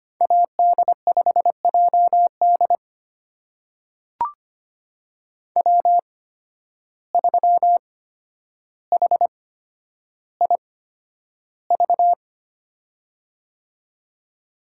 Morse code, heard as a keyed tone, about 25 wpm, spaced as for 11 wpm, 700 Hz. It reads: AD5JD W3HIV
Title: USA Call Signs - 5 Characters - 8x Character Spacing - 25wpm